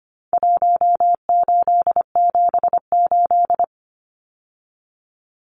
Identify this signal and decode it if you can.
Morse code, heard as a keyed tone, about 25 wpm, 700 Hz. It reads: 1878
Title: Numbers - Contest Style - 4-Digit 25wpm